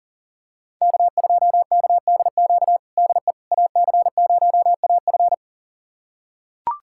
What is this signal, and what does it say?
Morse code, heard as a keyed tone, about 40 wpm, 700 Hz. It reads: K2KDQ DE AC0AF
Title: USA Callsign DE Callsign - 5 Characters - 40wpm